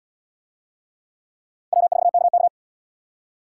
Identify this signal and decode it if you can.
Morse code, heard as a keyed tone, about 67 wpm, 700 Hz. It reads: W4KK